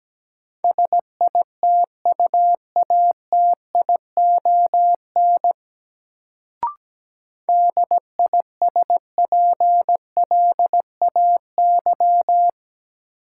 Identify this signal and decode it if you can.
Morse code, heard as a keyed tone, about 17 wpm, 700 Hz. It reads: SITUATION DISPLAY
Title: Sentences from Top 1200 Words - New Word Drill - 17wpm